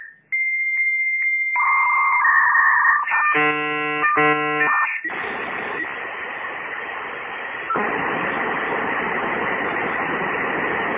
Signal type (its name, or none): dial-up modem handshake